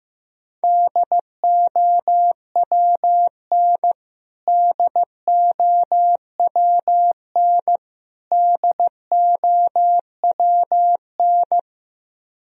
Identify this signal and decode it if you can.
Morse code, heard as a keyed tone, about 15 wpm, 700 Hz. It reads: DOWN DOWN DOWN
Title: Top 100 Words Repeated 15wpm